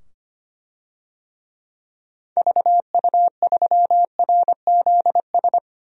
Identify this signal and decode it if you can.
Morse code, heard as a keyed tone, about 25 wpm, 700 Hz. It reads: VU3RZS
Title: DXCC-100 Call Signs - 6 Characters - 25wpm